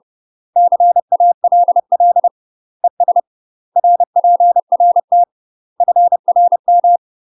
Morse code, heard as a keyed tone, about 30 wpm, 700 Hz. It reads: CALL ES RPRT FRM
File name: UK MASTER QSOs TRAINING / 30WPM / UK MASTER QSOs 0039